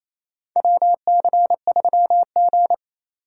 Morse code, heard as a keyed tone, about 28 wpm, 700 Hz. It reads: WC3G
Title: USA Call Signs - 4 Characters - 2x Character Spacing - 28wpm